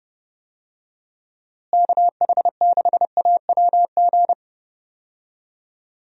Morse code, heard as a keyed tone, about 30 wpm, 700 Hz. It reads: KH6AWG
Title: DXCC-100 Call Signs - 6 Characters - Repeated 2x - 30wpm